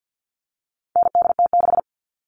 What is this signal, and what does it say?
Morse code, heard as a keyed tone, about 50 wpm, 700 Hz. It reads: NDT6
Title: All Cut Numbers - 4-Digit 50wpm